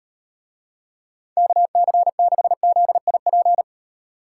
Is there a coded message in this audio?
Yes: Morse code from 1.4 to 3.6 s, heard as a keyed tone, about 38 wpm, 700 Hz. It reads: KC6ZIP